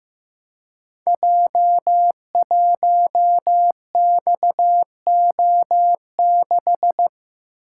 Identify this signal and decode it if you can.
Morse code, heard as a keyed tone, about 15 wpm, 700 Hz. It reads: J1XO6
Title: Code Groups - 5 Letter:Numbers - 15wpm